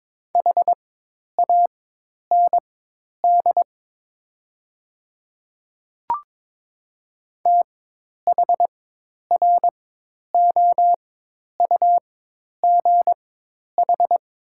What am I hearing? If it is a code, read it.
Morse code, heard as a keyed tone, about 22 wpm, spaced as for 10 wpm, 700 Hz. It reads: HAND THROUGH
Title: 4x Character Spacing - Top 200 Words - 22wpm